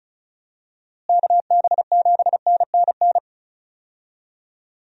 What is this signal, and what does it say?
Morse code, heard as a keyed tone, about 36 wpm, 700 Hz. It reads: KB7NNN
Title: USA Call Signs - 6 Characters - Repeated 2x - 35wpm